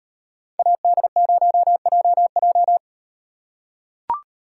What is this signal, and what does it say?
Morse code, heard as a keyed tone, about 38 wpm, 700 Hz. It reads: AD0JJ